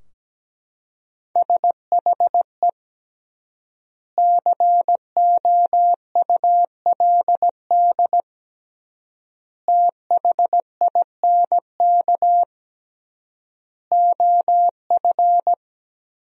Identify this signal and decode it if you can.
Morse code, heard as a keyed tone, about 17 wpm, 700 Hz. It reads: SHE COULD THINK OF
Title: 3x Word-Spacing - Sentences from Top 500 Words - 17wpm